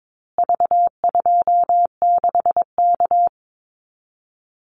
Morse code, heard as a keyed tone, about 22 wpm, 700 Hz. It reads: V26K
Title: Call Signs Repeated 2x - 22wpm